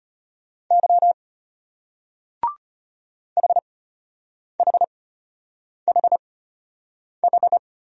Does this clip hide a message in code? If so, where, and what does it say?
Yes: Morse code from 0.7 to 7.6 s, heard as a keyed tone, about 33 wpm, spaced as for 8 wpm, 700 Hz. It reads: Y HHHH